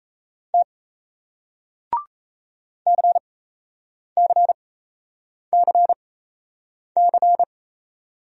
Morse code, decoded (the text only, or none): T CCCC